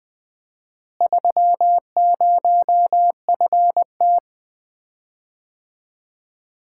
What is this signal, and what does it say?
Morse code, heard as a keyed tone, about 20 wpm, 700 Hz. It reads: 30FT
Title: Common QSO Phrases - 4x Word-Spacing - 20wpm